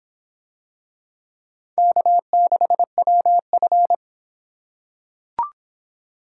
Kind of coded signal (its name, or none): Morse code